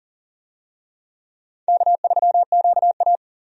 Morse code, heard as a keyed tone, about 40 wpm, 700 Hz. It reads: K3QA
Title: USA Call Signs - 4 Characters - 40wpm